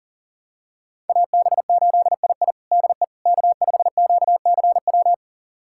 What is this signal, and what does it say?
Morse code, heard as a keyed tone, about 40 wpm, 700 Hz. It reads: AB8II DE K5QCW
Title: USA Callsign DE Callsign - 5 Characters - 40wpm